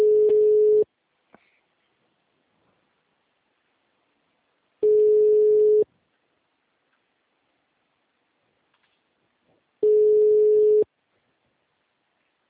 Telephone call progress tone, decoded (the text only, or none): ringback tone